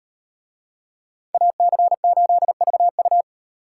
Morse code, heard as a keyed tone, about 38 wpm, 700 Hz. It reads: AC8VU